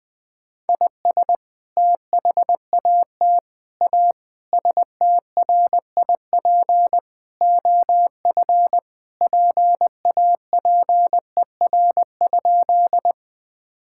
Morse code, heard as a keyed tone, about 20 wpm, 700 Hz. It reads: IS THAT A STRIP OF PAPER?
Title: Sentences from Top 1600 Words 20wpm